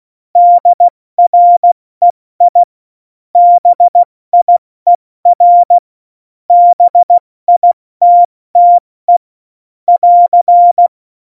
Morse code, heard as a keyed tone, about 16 wpm, 700 Hz. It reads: DREI BIER BITTE <AR>
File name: Morsekurs / Witze / Witz 15. Oktober 2019, 70 F80